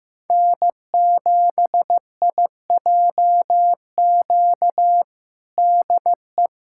Morse code, heard as a keyed tone, about 15 wpm, 700 Hz. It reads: N7IJQ DE
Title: USA Callsign DE Callsign 15wpm